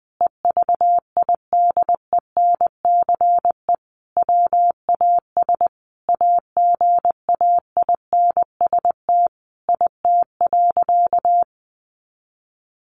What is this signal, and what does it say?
Morse code, heard as a keyed tone, about 20 wpm, 700 Hz. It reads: EVIDENCE WAS AGAINST IT.
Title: Sentences from Top 1400 Words 20wpm